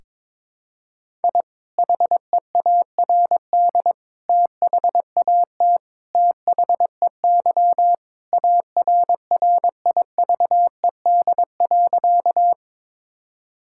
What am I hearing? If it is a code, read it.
Morse code, heard as a keyed tone, about 22 wpm, 700 Hz. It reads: I HEARD THAT THEY ARRIVED.
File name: practice / 22wpm / Sentences with 2-Word Phrases - 22wpm